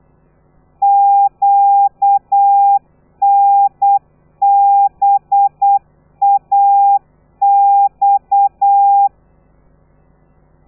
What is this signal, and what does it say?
Morse code, heard as a keyed tone, about 8 wpm, 800 Hz. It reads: QNBAX
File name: telegrafia / i2tfj / Lez13 35